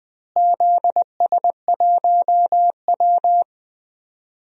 Morse code, heard as a keyed tone, about 20 wpm, 700 Hz. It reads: ZS1W
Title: DXCC-100 Call Signs - 4 Characters - Repeated 2x - 20wpm